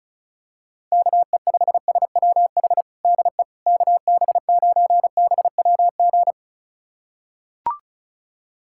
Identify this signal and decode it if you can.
Morse code, heard as a keyed tone, about 34 wpm, 700 Hz. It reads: KE5SWH DE KB9BWG
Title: USA Callsign DE Callsign - 6 Characters - 35wpm